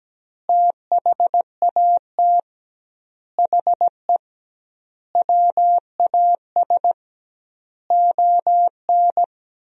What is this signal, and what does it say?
Morse code, heard as a keyed tone, about 17 wpm, 700 Hz. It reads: THAT HE WAS ON
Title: 2x Word-Spacing - Sentences from Top 500 Words - 17wpm